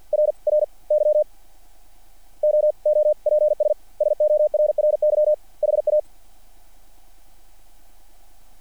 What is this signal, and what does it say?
Morse code, heard as a keyed tone, about 50 wpm, spaced as for 41 wpm, 600 Hz. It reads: RRY OOPS SORRY HA
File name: archived / CRSnet